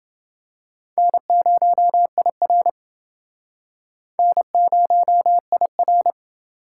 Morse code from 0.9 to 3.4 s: N0IR

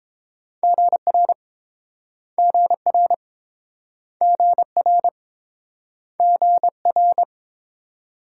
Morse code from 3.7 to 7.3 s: GR GR

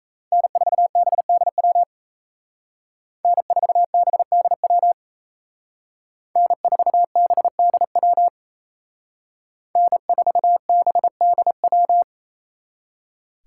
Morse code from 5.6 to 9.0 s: N4BDW

